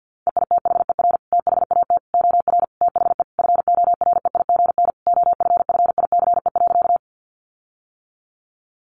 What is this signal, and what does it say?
Morse code, heard as a keyed tone, about 50 wpm, 700 Hz. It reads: EITHER THAT OR THE FOREIGN OFFICE.
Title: Sentences from Top 1400 Words 50wpm